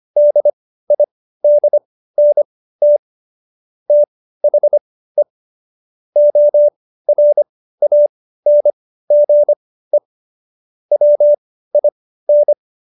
Morse code, heard as a keyed tone, about 25 wpm, spaced as for 15 wpm, 590 Hz. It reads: DIDNT THE ORANGE WIN
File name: Jokes-25-char-15-eff